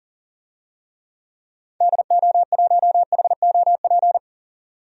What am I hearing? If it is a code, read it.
Morse code, heard as a keyed tone, about 40 wpm, 700 Hz. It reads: DO1HOP
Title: DXCC-100 Call Signs - 6 Characters - 40wpm